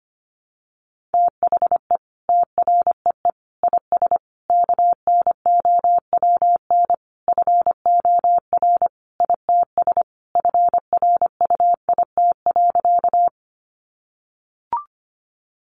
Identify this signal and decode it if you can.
Morse code, heard as a keyed tone, about 25 wpm, 700 Hz. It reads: THE TREE IS KNOWN FOR ITS FRUIT.
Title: Sentences from Top 1300 Words 25wpm